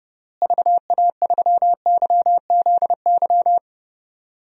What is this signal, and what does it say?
Morse code, heard as a keyed tone, about 32 wpm, 700 Hz. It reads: VA3YZY
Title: DXCC-100 Call Signs - 6 Characters - Repeated 2x - 30wpm